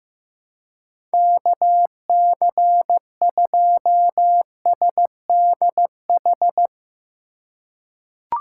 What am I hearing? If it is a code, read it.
Morse code, heard as a keyed tone, about 15 wpm, 700 Hz. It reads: KC2SDH
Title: USA Call Signs - 6 Characters - 2x Character Spacing - 15wpm